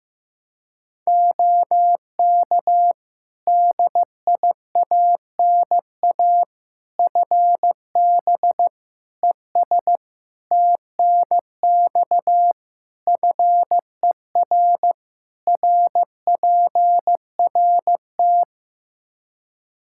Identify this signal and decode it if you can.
Morse code, heard as a keyed tone, about 15 wpm, 700 Hz. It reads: OK DIANA FB ES TNX FER RPRT
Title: Practice QSO - Exchange 2 - Info - 15wpm